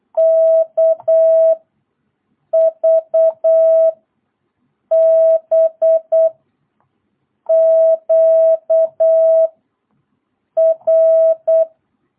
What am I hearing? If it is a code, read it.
Morse code, heard as a keyed tone, about 8 wpm, 660 Hz. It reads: K V B Q R
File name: jh8cbh / 2022cw